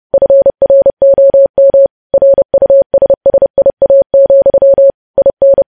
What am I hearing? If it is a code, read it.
Morse code, heard as a keyed tone, about 30 wpm, 550 Hz. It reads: FROM RUSSIA, IN
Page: news headlines in morse code at 30 wpm